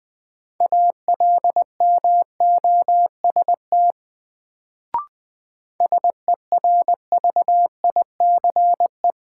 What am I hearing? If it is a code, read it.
Morse code, heard as a keyed tone, about 21 wpm, 700 Hz. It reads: ALMOST SERVICE
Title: Sentences from Top 700 Words - New Word Drill - 20wpm